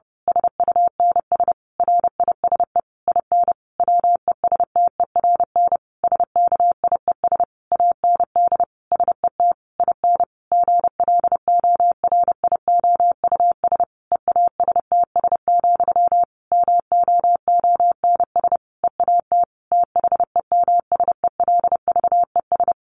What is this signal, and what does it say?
Morse code, heard as a keyed tone, about 30 wpm, 700 Hz. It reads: SUNS RISE IN WESTERN SKIES AND SET IN GLORIOUS EASTS, MOONS EAT THEMSELVES
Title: A Sound Of Thunder 30wpm